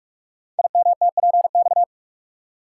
Morse code, heard as a keyed tone, about 44 wpm, 690 Hz. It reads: IMTPX
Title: Code Groups - 5 Letters - 45wpm